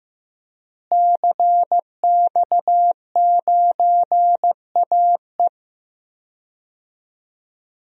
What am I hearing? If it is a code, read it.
Morse code, heard as a keyed tone, about 15 wpm, 700 Hz. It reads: CX9AE
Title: DXCC-100 Call Signs - 5 Characters - Repeated 2x - 15wpm